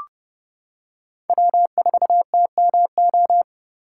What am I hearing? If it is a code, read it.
Morse code, heard as a keyed tone, about 31 wpm, 700 Hz. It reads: W4TMO
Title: K1USN SST QSO Practice - 30wpm